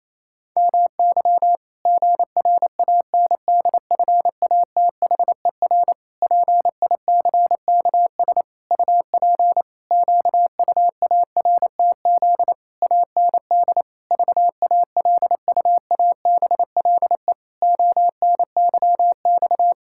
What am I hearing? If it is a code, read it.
Morse code, heard as a keyed tone, about 28 wpm, 700 Hz. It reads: MY GRANDFATHER PICKS UP QUARTZ AND VALUABLE ONYX
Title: Pangram Challenge v2 - 28wpm